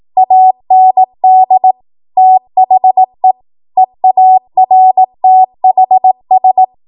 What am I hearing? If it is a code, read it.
Morse code, heard as a keyed tone, about 18 wpm, 750 Hz. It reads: AND THE EARTHS
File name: Morse / 18WPM